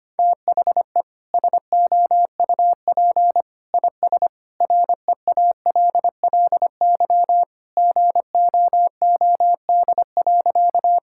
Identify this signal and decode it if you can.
Morse code, heard as a keyed tone, about 25 wpm, 700 Hz. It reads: THE SOUP IS REALLY GOOD.
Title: Sentences from Top 1000 Words - Review Part 6 - 25wpm